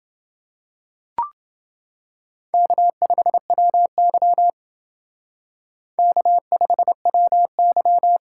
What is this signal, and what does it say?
Morse code, heard as a keyed tone, about 28 wpm, 700 Hz. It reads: K5WY K5WY